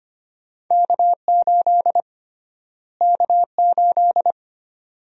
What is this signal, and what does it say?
Morse code, heard as a keyed tone, about 25 wpm, 700 Hz. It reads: K8 K8
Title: USA Call Sign Prefixes - Repeated 2x - 25wpm